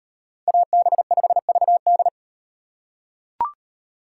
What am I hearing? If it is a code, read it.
Morse code, heard as a keyed tone, about 38 wpm, 700 Hz. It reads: AB5VD